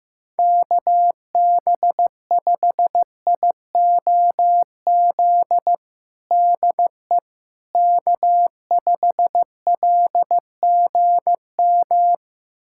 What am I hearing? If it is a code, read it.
Morse code, heard as a keyed tone, about 15 wpm, 700 Hz. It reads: KB5IOZ DE K5LGM
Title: USA Callsign DE Callsign 15wpm